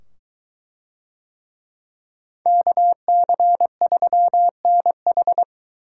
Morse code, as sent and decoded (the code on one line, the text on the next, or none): -.- -.-. ...-- -. ....
KC3NH